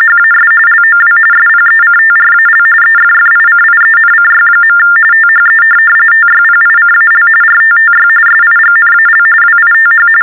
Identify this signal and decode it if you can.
TTY/TDD text telephone tones, about 45.45 baud. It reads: HAS LOW SPECTRAL EFFICIENCY. THE TYPICAL RTTY SIGNAL WITH
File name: teletype-RTTY-45